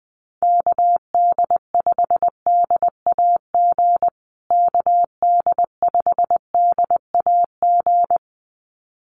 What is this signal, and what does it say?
Morse code, heard as a keyed tone, about 20 wpm, 700 Hz. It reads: KD5DAG KD5DAG
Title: USA Call Signs Repeated 2x 20wpm